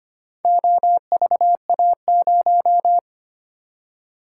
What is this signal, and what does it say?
Morse code, heard as a keyed tone, about 25 wpm, 700 Hz. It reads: OVA0
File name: practice / 25wpm / Four Letter-Number Combinations - 25wpm